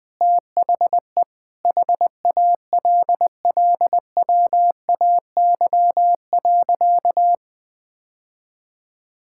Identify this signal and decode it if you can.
Morse code, heard as a keyed tone, about 20 wpm, 700 Hz. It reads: THE HALLWAY.
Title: Sentences with 4-Word Phrases - 20wpm